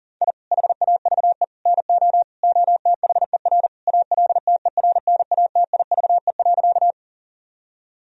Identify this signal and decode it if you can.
Morse code, heard as a keyed tone, about 37 wpm, 700 Hz. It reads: I HAVE NO OTHER ALTERNATIVE.
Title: Sentences from Top 1000 Words - Review Part 4 - 40wpm